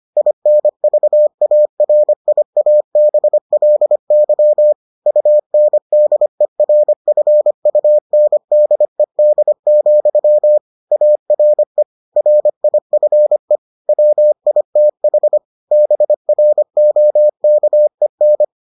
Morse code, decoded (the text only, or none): INVARIABLY UNDERFUNDED, ARE RIFE WITH BROKEN